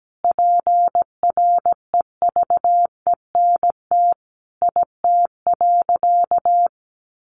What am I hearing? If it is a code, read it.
Morse code, heard as a keyed tone, about 17 wpm, 700 Hz. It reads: PREVENT IT.
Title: Sentences from Top 1100 Words 17wpm